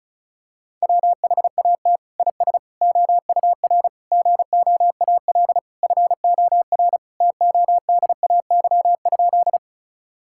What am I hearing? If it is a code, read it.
Morse code, heard as a keyed tone, about 34 wpm, 700 Hz. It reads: WHAT IS OUR GOAL FOR TODAY?
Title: Sentences from Top 1000 Words - Review Part 5 - 35wpm